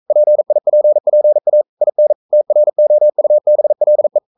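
Morse code, heard as a keyed tone, about 42 wpm, 600 Hz. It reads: PIPPA IN TROUBLE
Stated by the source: Morse code sending of an English novel at 42 wpm.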